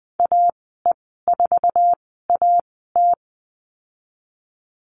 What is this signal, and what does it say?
Morse code, heard as a keyed tone, about 20 wpm, spaced as for 15 wpm, 700 Hz. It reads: AE4AT
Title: 048 Lesson 9g - Number 4 - Call Signs 20wpm